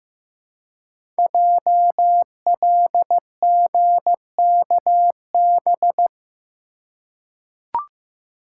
Morse code, heard as a keyed tone, about 15 wpm, 700 Hz. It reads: JLGKB
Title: Code Groups - 5 Letters - 15wpm